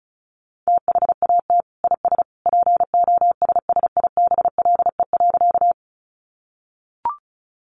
Morse code, heard as a keyed tone, about 34 wpm, 700 Hz. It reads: THAT IS POSSIBLE.
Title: Sentences from Top 500 Words 35wpm